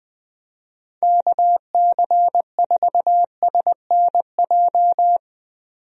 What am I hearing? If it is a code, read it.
Morse code, heard as a keyed tone, about 20 wpm, 700 Hz. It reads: KC4SNJ
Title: USA Call Signs - 6 Characters - 20wpm